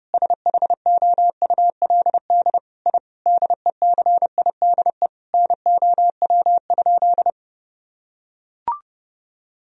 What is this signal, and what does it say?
Morse code, heard as a keyed tone, about 30 wpm, 700 Hz. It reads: SHOULD I DECIDE NOW?
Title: Sentences from Top 1000 Words - Review Part 3 - 30wpm